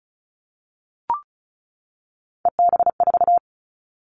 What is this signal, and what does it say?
Morse code, heard as a keyed tone, about 34 wpm, 700 Hz. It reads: EB4
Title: Callsign Prefixes for Top 25 DXCC - 35wpm